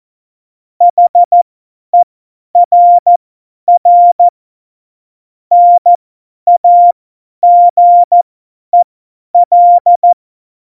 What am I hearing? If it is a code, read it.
Morse code, heard as a keyed tone, about 14 wpm, spaced as for 10 wpm, 700 Hz. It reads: HERR NAGEL
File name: Morsekurs / Witze / Witz 15. September 2020, 50 F70